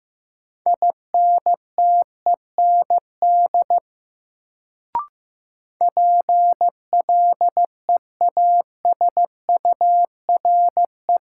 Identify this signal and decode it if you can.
Morse code, heard as a keyed tone, about 15 wpm, 700 Hz. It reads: INTEND PLEASURE
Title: Sentences from Top 1600 Words - New Word Drill Alt - 15wpm